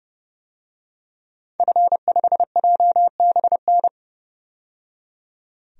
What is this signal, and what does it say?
Morse code, heard as a keyed tone, about 30 wpm, 700 Hz. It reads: F5JBN
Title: DXCC-100 Call Signs - 5 Characters - 30wpm